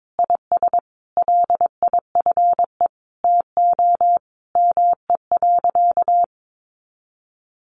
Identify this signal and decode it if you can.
Morse code, heard as a keyed tone, about 22 wpm, 700 Hz. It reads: IS LIFE TO ME.
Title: Sentences from Top 500 Words 22wpm